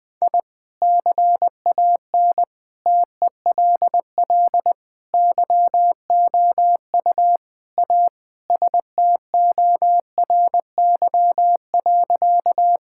Morse code, heard as a keyed tone, about 21 wpm, 700 Hz. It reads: I CAN TELL YOU A STORY.
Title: Sentences with 4-Word Phrases - Spoken Beginning - 20wpm